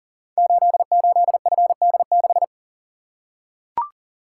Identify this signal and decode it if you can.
Morse code, heard as a keyed tone, about 38 wpm, 690 Hz. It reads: 88FD6
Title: Code Groups - 5 Letter:Numbers - 40wpm